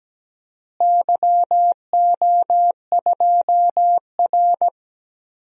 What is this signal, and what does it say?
Morse code, heard as a keyed tone, about 17 wpm, 700 Hz. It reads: YO2R
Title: DXCC-100 Call Signs - 4 Characters - Repeated 3x - 17wpm